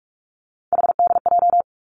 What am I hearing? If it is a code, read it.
Morse code, heard as a keyed tone, about 46 wpm, 700 Hz. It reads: HDJ
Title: Three Challenging Letter-Number Combinations 45wpm